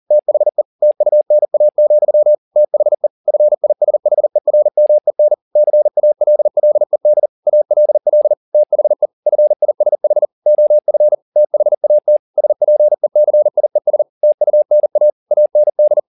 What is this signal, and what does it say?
Morse code, heard as a keyed tone, about 40 wpm, 600 Hz. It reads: THE TUNA, THE FISHERMEN CALLED ALL THE FISH OF THAT SPECIES TUNA AND